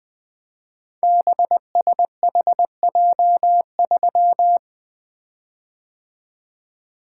Morse code, heard as a keyed tone, about 20 wpm, 700 Hz. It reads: BSHJ3
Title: Code Groups - 5 Letter:Numbers - 20wpm